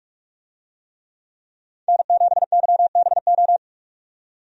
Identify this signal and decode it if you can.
Morse code, heard as a keyed tone, about 46 wpm, 700 Hz. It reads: N7YBO
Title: USA Call Signs - 5 Characters - 45wpm